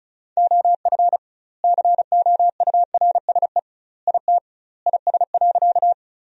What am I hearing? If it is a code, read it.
Morse code, heard as a keyed tone, about 34 wpm, 700 Hz. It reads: OF COURSE IT IS.
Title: 2x Word-Spacing - Sentences from Top 500 Words - 35wpm